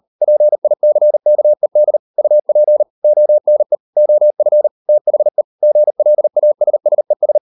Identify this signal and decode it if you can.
Morse code, heard as a keyed tone, about 38 wpm, 600 Hz. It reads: PICKED UP ONE OF THE GLASSES